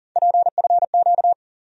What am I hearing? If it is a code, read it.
Morse code, heard as a keyed tone, about 40 wpm, 700 Hz. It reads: PFQ
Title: USA Call Sign Postfixes - 3 Letters 40wpm